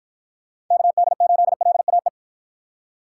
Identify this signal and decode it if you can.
Morse code, heard as a keyed tone, about 53 wpm, 700 Hz. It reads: KD8LAE